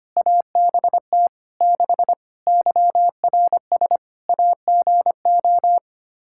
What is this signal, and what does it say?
Morse code, heard as a keyed tone, about 25 wpm, 700 Hz. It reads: ABT 6 YRS AGO
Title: Practice QSO - Exchange 3 - More Info - 25wpm